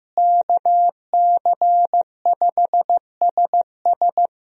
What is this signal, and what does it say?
Morse code, heard as a keyed tone, about 15 wpm, 700 Hz. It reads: KC5SS
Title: USA Call Signs - 5 Characters - Repeated 2x - 15wpm